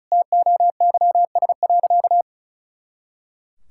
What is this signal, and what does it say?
Morse code, heard as a keyed tone, about 34 wpm, 700 Hz. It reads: TOYS.
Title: Pangram Challenge v2 - 35wpm